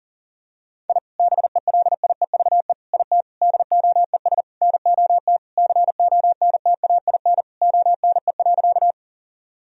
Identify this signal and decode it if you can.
Morse code, heard as a keyed tone, about 40 wpm, 700 Hz. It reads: I BELIEVE IT DOES NOT CONTAIN ONE.
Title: Sentences from Top 1000 Words - Review Part 2 - 40wpm